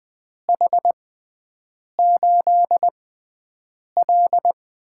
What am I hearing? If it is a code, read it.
Morse code, heard as a keyed tone, about 21 wpm, spaced as for 7 wpm, 700 Hz. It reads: H8L
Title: USA Call Signs - 4 Characters - 6x Character Spacing - 20wpm